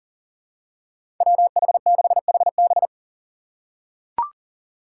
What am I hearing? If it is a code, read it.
Morse code, heard as a keyed tone, about 40 wpm, 700 Hz. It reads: WH6HB
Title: USA Call Signs - 5 Characters - 4x Character Spacing - 40wpm